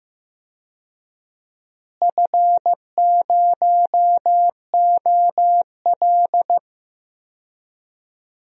Morse code, heard as a keyed tone, about 15 wpm, 700 Hz. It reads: F0OL
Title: DXCC-100 Call Signs - 4 Characters - Repeated 2x - 15wpm